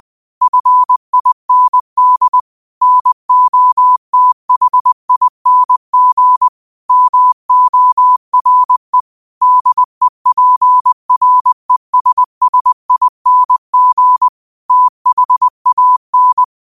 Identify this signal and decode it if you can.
Morse code, heard as a keyed tone, about 20 wpm, 1000 Hz. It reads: FIND NOTHING MORE DEPRESSING THAN